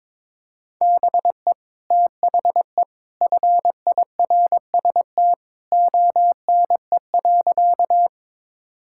Morse code, heard as a keyed tone, about 22 wpm, 700 Hz. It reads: BE THE FIRST ONE.
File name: practice / 22wpm / Sentences with 2-Word Phrases - 22wpm